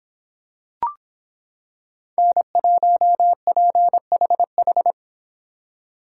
Morse code, heard as a keyed tone, about 26 wpm, 700 Hz. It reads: N1PHH